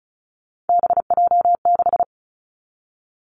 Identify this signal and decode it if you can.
Morse code, heard as a keyed tone, about 36 wpm, 700 Hz. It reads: BJ6
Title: Three Challenging Letter-Number Combinations 35wpm